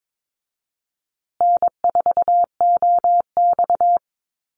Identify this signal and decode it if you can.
Morse code, heard as a keyed tone, about 22 wpm, 700 Hz. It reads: N4OX
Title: Easy Call Signs Repeated 3x - 22wpm